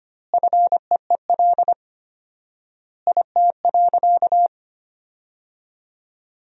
Morse code, heard as a keyed tone, about 25 wpm, 700 Hz. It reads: FEEL IT.
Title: 4x Word-Spacing - Sentences from Top 400 Words - 25wpm